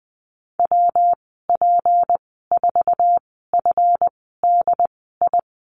Morse code, heard as a keyed tone, about 20 wpm, spaced as for 15 wpm, 700 Hz. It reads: WP4FDI
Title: 154 Lesson 25g - Letter p - Call Signs 20wpm